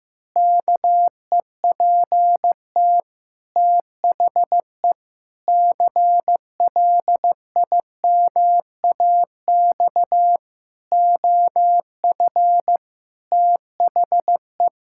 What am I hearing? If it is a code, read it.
Morse code, heard as a keyed tone, about 16 wpm, 700 Hz. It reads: KEPT THE CLIMAX OF THE
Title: Pangram Challenge v2 - 15wpm